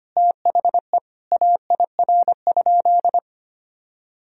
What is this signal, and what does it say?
Morse code, heard as a keyed tone, about 25 wpm, 700 Hz. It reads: THE AIR?
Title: Sentences from Top 1000 Words - Review Part 2 - 25wpm